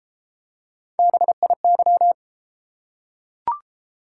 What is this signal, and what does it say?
Morse code, heard as a keyed tone, about 31 wpm, 700 Hz. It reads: BIY